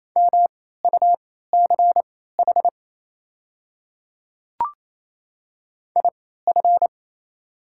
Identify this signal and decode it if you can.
Morse code, heard as a keyed tone, about 28 wpm, spaced as for 16 wpm, 700 Hz. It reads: MUCH IF
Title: 3x Character Spacing - Top 200 Words - 28wpm